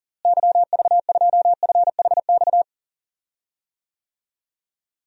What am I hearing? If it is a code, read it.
Morse code, heard as a keyed tone, about 40 wpm, 700 Hz. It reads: YV2FHX
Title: DXCC-100 Call Signs - 6 Characters - Repeated 2x - 40wpm